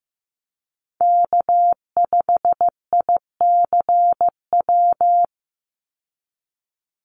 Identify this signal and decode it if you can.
Morse code, heard as a keyed tone, about 15 wpm, 700 Hz. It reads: K5ICW
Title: USA Call Signs 15wpm